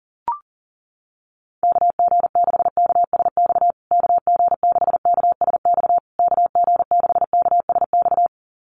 Morse code, heard as a keyed tone, about 40 wpm, 700 Hz. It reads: KG6KSX KG6KSX KG6KSX
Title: USA Call Signs Repeated 3x 40wpm